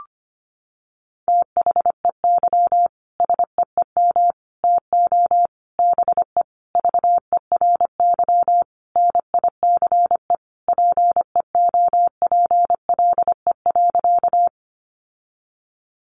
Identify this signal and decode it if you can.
Morse code, heard as a keyed tone, about 25 wpm, 700 Hz. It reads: THEY SEEM TO BE VERY NICE PEOPLE.
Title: Sentences from Top 1100 Words 25wpm